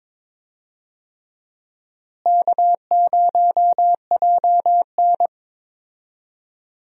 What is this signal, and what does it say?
Morse code, heard as a keyed tone, about 22 wpm, 700 Hz. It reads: K0JN
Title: USA Call Signs - 4 Characters - Repeated 3x - 22wpm